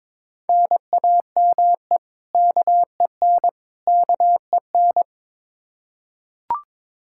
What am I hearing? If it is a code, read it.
Morse code, heard as a keyed tone, about 21 wpm, 700 Hz. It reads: NAME KEN KEN
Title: Practice QSO - Exchange 1 - Beginning - 22wpm